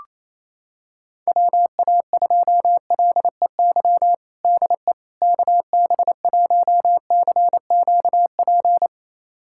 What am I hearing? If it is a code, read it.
Morse code, heard as a keyed tone, about 28 wpm, 700 Hz. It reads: WA2LEY DE KB1CQP
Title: USA Callsign DE Callsign 28wpm